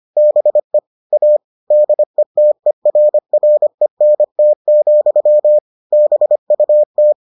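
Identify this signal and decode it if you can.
Morse code, heard as a keyed tone, about 25 wpm, 600 Hz. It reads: BE A DETERRENT, BUT
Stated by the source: Morse edition of the Foundations of Amateur Radio episodes.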